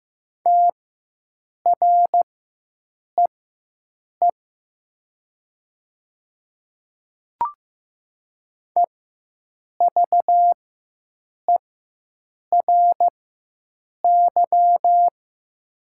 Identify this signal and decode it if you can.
Morse code, heard as a keyed tone, about 15 wpm, spaced as for 7 wpm, 700 Hz. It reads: TREE EVERY
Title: 4x Character Spacing - Top 300 Words - 15wpm